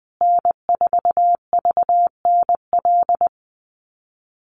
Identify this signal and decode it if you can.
Morse code, heard as a keyed tone, about 20 wpm, 700 Hz. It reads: N4VNL
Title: USA Call Signs Repeated 3x 20wpm